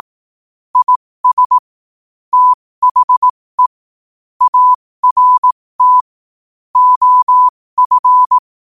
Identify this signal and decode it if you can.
Morse code, heard as a keyed tone, about 18 wpm, spaced as for 15 wpm, 1000 Hz. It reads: IS THE ART OF